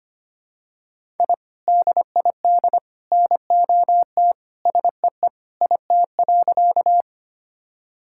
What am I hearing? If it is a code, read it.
Morse code, heard as a keyed tone, about 25 wpm, 700 Hz. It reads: I DID NOT SEE IT.
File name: practice / 25wpm / Sentences with 3-Word Phrases - 25wpm